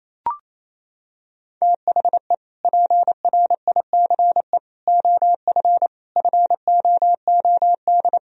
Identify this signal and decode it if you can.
Morse code, heard as a keyed tone, about 28 wpm, 700 Hz. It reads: THE PRICE OF FOOD
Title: Sentences from Top 1000 Words - Review Part 4 - 28wpm